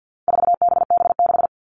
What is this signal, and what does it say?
Morse code, heard as a keyed tone, about 50 wpm, 700 Hz. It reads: 4BB6
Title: All Cut Numbers - 4-Digit 50wpm